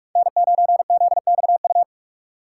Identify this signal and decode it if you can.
Morse code, heard as a keyed tone, about 46 wpm, 700 Hz. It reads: N9ZXU
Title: USA Call Signs - 5 Characters - 4x Character Spacing - 45wpm